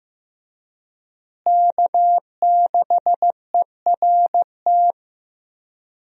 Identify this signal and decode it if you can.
Morse code, heard as a keyed tone, about 15 wpm, 700 Hz. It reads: K6ERT
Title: USA Call Signs - 5 Characters - Repeated 3x - 15wpm